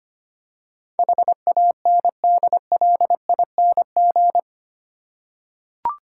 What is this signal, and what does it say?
Morse code, heard as a keyed tone, about 25 wpm, 700 Hz. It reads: HANDLING
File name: Eight Letter Words - 25wpm